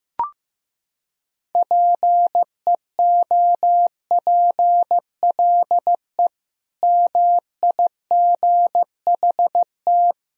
Morse code, heard as a keyed tone, about 16 wpm, 700 Hz. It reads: PEOPLE MIGHT
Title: Sentences from Top 500 Words and Abbreviations 15wpm